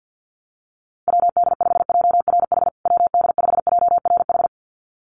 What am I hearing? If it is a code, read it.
Morse code, heard as a keyed tone, about 50 wpm, 700 Hz. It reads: WD5JRH WD5JRH
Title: USA Call Signs Repeated 2x 50wpm